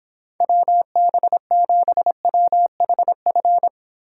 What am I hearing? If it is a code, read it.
Morse code, heard as a keyed tone, about 29 wpm, 700 Hz. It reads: WB7WHF